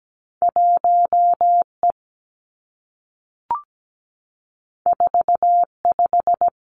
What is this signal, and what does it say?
Morse code, heard as a keyed tone, about 17 wpm, 700 Hz. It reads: 1 E 4 5
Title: Two Challenging Letter-Number Combinations 17wpm